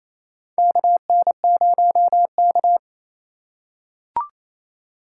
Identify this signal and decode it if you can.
Morse code, heard as a keyed tone, about 27 wpm, 700 Hz. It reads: KN0K
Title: USA Call Signs - 4x Character Spacing - 28wpm